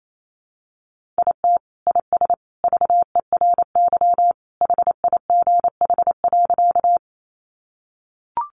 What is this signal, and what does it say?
Morse code, heard as a keyed tone, about 28 wpm, 700 Hz. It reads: IT IS VERY HIGH.
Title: Sentences from Top 200 Words 28wpm